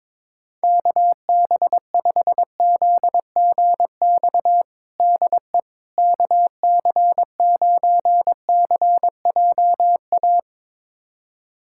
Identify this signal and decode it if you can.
Morse code, heard as a keyed tone, about 22 wpm, 700 Hz. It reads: KB5ZGX DE KC9CJA
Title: USA Callsign DE Callsign - 6 Characters - 22wpm